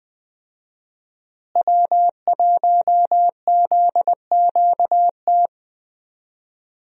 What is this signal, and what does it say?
Morse code, heard as a keyed tone, about 20 wpm, 700 Hz. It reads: W1ZQT
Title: USA Call Signs - 5 Characters - Repeated 2x - 20wpm